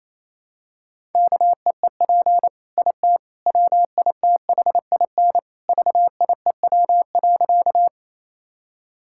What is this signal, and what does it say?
Morse code, heard as a keyed tone, about 28 wpm, 700 Hz. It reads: KEEP IT WITHIN VIEW.
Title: Sentences from Top 1000 Words - Review Part 6 - 28wpm